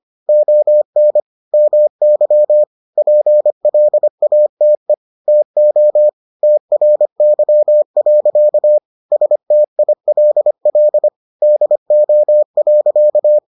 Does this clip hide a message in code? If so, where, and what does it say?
Yes: Morse code from 0.3 to 13.4 s, heard as a keyed tone, about 25 wpm, 600 Hz. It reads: ON MY PLATE TO TRY. STILL DO.